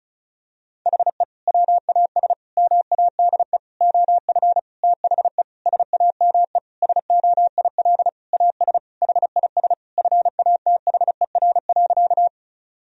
Morse code, heard as a keyed tone, about 36 wpm, 700 Hz. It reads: HE WAS MADE OF THE SAME SOIL AS HIS FATHER.
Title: Sentences from Top 1000 Words - Review Part 4 - 35wpm